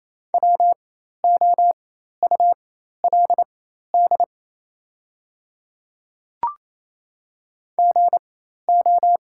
Morse code, heard as a keyed tone, about 28 wpm, spaced as for 13 wpm, 700 Hz. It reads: WOULD GO
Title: 4x Character Spacing - Top 100 Words - 28wpm